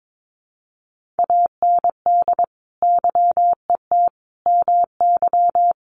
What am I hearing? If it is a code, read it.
Morse code, heard as a keyed tone, about 22 wpm, 700 Hz. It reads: AND YET MY
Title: Sentences from Top 1200 Words 22wpm